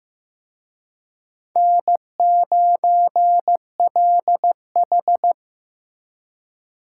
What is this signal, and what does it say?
Morse code, heard as a keyed tone, about 18 wpm, 700 Hz. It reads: N9LH
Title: Speed Racing - USA 1x2 Call Signs 15wpm